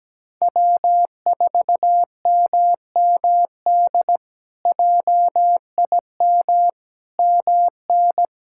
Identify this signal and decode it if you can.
Morse code, heard as a keyed tone, about 17 wpm, 700 Hz. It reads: W4MMD JIM MN
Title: K1USN SST QSO Practice - 17wpm